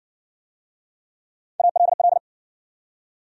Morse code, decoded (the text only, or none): ALL